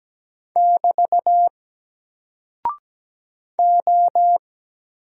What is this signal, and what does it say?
Morse code, heard as a keyed tone, about 17 wpm, 700 Hz. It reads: <BT> O
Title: Single Character (Letter:Number:Punctuation:Prosign) 17wpm